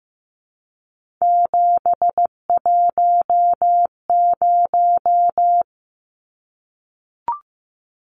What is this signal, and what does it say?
Morse code, heard as a keyed tone, about 15 wpm, 700 Hz. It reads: 710
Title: Numbers - 3-Digit 15wpm